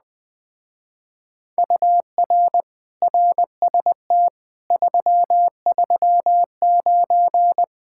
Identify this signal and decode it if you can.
Morse code, heard as a keyed tone, about 20 wpm, 700 Hz. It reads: UR RST 339
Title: Speed-Racing Common QSO Phrases - 17wpm